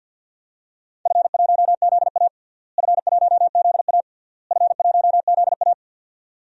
Morse code, heard as a keyed tone, about 52 wpm, 700 Hz. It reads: F1ZA F1ZA F1ZA
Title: DXCC-100 Call Signs - 4 Characters - Repeated 3x - 50wpm